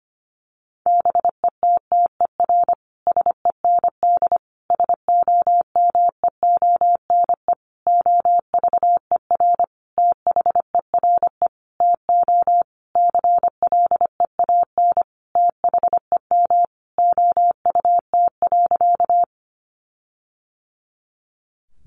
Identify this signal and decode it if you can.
Morse code, heard as a keyed tone, about 25 wpm, 700 Hz. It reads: BETTER SEND SOMEONE OVER THERE TO CLEAN THEM OUT.
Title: Sentences from Top 1500 Words 25wpm